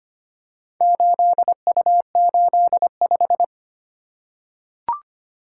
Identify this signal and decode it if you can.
Morse code, heard as a keyed tone, about 25 wpm, 700 Hz. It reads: 8U85
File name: practice / 25wpm / Four Letter-Number Combinations - 25wpm